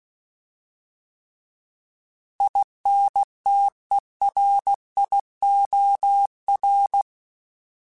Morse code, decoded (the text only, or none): INTERIOR